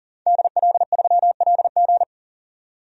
Morse code, heard as a keyed tone, about 40 wpm, 700 Hz. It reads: DL3LG
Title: DXCC-100 Call Signs - 5 Characters - 40wpm